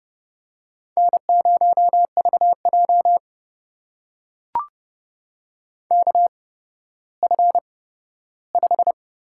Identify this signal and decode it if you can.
Morse code, heard as a keyed tone, about 32 wpm, spaced as for 15 wpm, 700 Hz. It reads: N0VJ KF5
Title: USA Call Signs - 8x Character Spacing - 30wpm